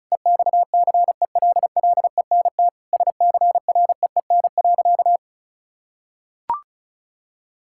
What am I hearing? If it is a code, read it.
Morse code, heard as a keyed tone, about 34 wpm, 700 Hz. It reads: EXCELLENT SCREEN.
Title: Sentences from Top 1000 Words - Review Part 6 - 35wpm